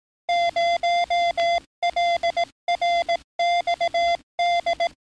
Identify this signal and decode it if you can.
Morse code, heard as a keyed tone, about 18 wpm, 690 Hz. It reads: 0LRXD
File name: training / NAVPERS 11049to11060 / NAVPERS 11057 RA RB